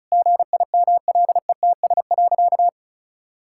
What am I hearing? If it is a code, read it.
Morse code, heard as a keyed tone, about 34 wpm, 700 Hz. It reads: GIMLETS.
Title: Pangram Challenge v2 - 35wpm